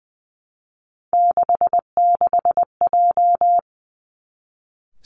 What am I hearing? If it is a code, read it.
Morse code, heard as a keyed tone, about 20 wpm, 700 Hz. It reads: 66J
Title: Three Challenging Letter-Number Combinations 20wpm